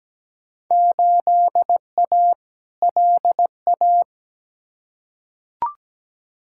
Morse code, heard as a keyed tone, about 17 wpm, 700 Hz. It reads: 8A LA
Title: Field Day Exchanges Encore - 17wpm